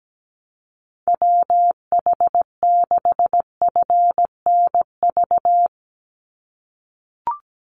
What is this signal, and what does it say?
Morse code, heard as a keyed tone, about 17 wpm, 700 Hz. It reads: WH6FNV
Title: USA Call Signs 17wpm